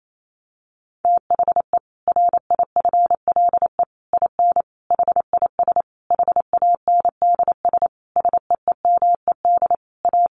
Morse code, heard as a keyed tone, about 28 wpm, 700 Hz. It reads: THE RIFLE IN HIS HANDS SEEMED A
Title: A Sound Of Thunder 28wpm